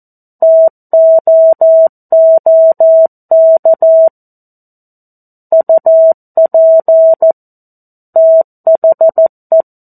Morse code, heard as a keyed tone, about 14 wpm, 650 Hz. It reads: TOOK UP THE
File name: 02-Chapter-02 14 WPM